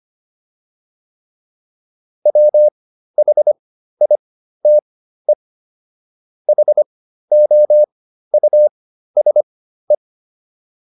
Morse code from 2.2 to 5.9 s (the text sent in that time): WHITE